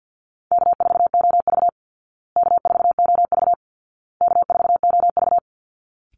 Morse code, heard as a keyed tone, about 48 wpm, 700 Hz. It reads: K4OV K4OV K4OV
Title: Easy Call Signs Repeated 3x - 50wpm